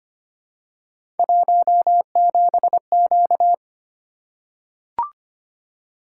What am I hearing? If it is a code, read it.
Morse code, heard as a keyed tone, about 25 wpm, 700 Hz. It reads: 17Q
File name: Three Letter-Number Combinations - 25wpm